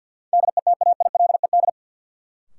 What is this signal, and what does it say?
Morse code, heard as a keyed tone, about 53 wpm, 700 Hz. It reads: DETAILED
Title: Sentences from Top 1500 Words - New Word Drill Alt - 50wpm